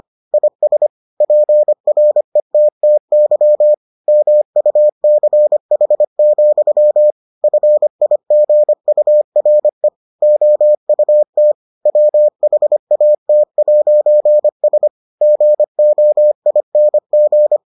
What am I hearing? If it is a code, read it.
Morse code, heard as a keyed tone, about 25 wpm, 600 Hz. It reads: IS PRETTY MUCH, FIGURE OUT WHAT'S GOING